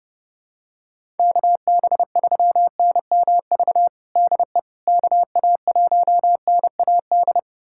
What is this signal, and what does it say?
Morse code, heard as a keyed tone, about 31 wpm, 700 Hz. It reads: KB3NMV DE KA1NAD
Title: USA Callsign DE Callsign - 6 Characters - 30wpm